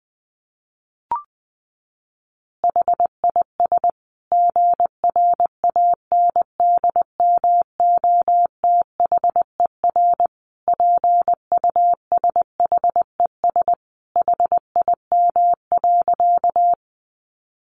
Morse code, heard as a keyed tone, about 20 wpm, 700 Hz. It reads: HIS GRANDMOTHER PUSHES HIM.
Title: Additional Easy Phrases 20wpm